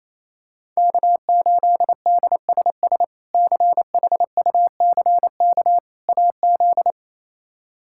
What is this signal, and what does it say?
Morse code, heard as a keyed tone, about 28 wpm, 700 Hz. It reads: K8DSS CHUCK AZ
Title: K1USN SST QSO Practice - 28wpm